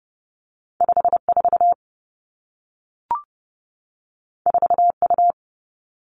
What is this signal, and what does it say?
Morse code, heard as a keyed tone, about 30 wpm, 700 Hz. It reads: 5 4 4 U
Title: Two Challenging Letter-Number Combinations 30wpm